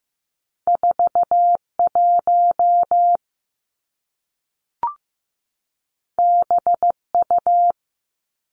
Morse code, heard as a keyed tone, about 15 wpm, 700 Hz. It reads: 4 1 B U
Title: Two Challenging Letter-Number Combinations 15wpm